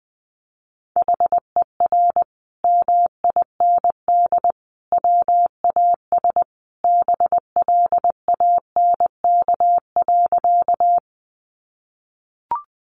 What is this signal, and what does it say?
Morse code, heard as a keyed tone, about 20 wpm, 700 Hz. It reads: HER MIND WAS BLANK.
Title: Sentences from Top 1500 Words 20wpm